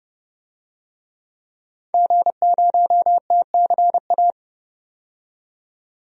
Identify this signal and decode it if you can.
Morse code, heard as a keyed tone, about 30 wpm, 700 Hz. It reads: G0TCA
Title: DXCC-100 Call Signs - 5 Characters - 30wpm